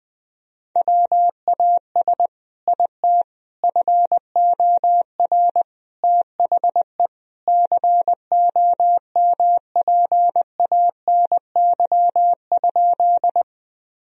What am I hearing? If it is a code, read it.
Morse code, heard as a keyed tone, about 20 wpm, 700 Hz. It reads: WAS IT FOR THE COMPANY?
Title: Sentences from Top 1000 Words - Review Part 3 - 20wpm